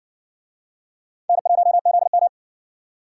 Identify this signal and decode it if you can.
Morse code, heard as a keyed tone, about 70 wpm, 700 Hz. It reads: N1ZM